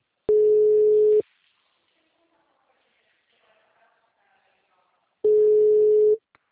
Ringback tone (Europe, 425 Hz).
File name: out- -2001- - - .157